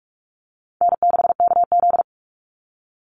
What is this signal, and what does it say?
Morse code, heard as a keyed tone, about 44 wpm, 700 Hz. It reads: N6KZ
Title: NA SOTA Chaser Callsigns 45wpm